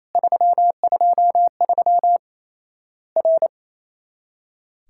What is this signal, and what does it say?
Morse code, heard as a keyed tone, about 28 wpm, 700 Hz. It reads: 323 R
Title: Worked All Europe DX Contest - QTC Exchange - 28wpm